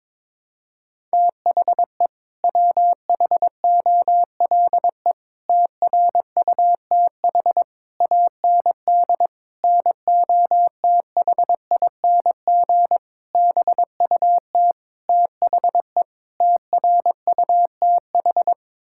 Morse code, heard as a keyed tone, about 22 wpm, 700 Hz. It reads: THE WHOLE TRUTH AND NOTHING BUT THE TRUTH
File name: practice / 22wpm / Sets of 8 Words - 22wpm